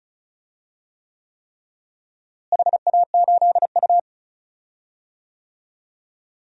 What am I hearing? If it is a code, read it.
Morse code, heard as a keyed tone, about 36 wpm, 700 Hz. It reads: HA8U
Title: DXCC-100 Call Signs - 4 Characters - Repeated 3x - 35wpm